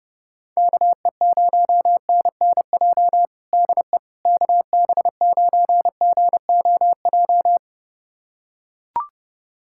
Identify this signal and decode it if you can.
Morse code, heard as a keyed tone, about 30 wpm, 700 Hz. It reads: KE0NNJ DE KB9GOJ
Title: USA Callsign DE Callsign - 6 Characters - 30wpm